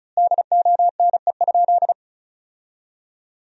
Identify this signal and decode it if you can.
Morse code, heard as a keyed tone, about 36 wpm, 700 Hz. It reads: DONE?
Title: 2x Word-Spacing - Sentences from Top 500 Words - 35wpm